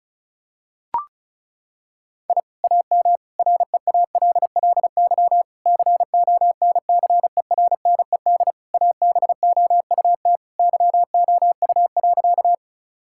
Morse code, heard as a keyed tone, about 36 wpm, 700 Hz. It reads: I AM REALLY CONCERNED ABOUT YOU.
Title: Sentences from Top 1000 Words - Review Part 6 - 35wpm